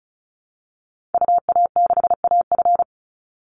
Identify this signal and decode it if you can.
Morse code, heard as a keyed tone, about 34 wpm, 700 Hz. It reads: UA6AF
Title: Call Signs Repeated 3x - 35wpm